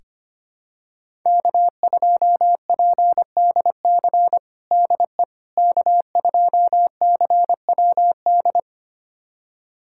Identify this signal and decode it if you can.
Morse code, heard as a keyed tone, about 25 wpm, 700 Hz. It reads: K2PDC DE K2CWD
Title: USA Callsign DE Callsign - 5 Characters - 25wpm